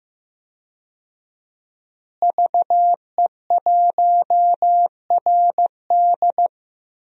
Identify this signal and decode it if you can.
Morse code, heard as a keyed tone, about 15 wpm, 700 Hz. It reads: VE1RD
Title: DXCC-100 Call Signs - 5 Characters - Repeated 3x - 15wpm